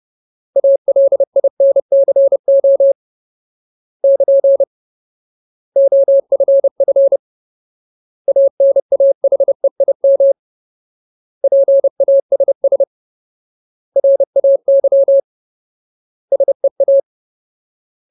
Morse code, heard as a keyed tone, about 30 wpm, 550 Hz. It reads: ALINCO <KN> OFF ANAHEIM PASS RAY SEA
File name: Ultimate Word List / 30WPM Ultimate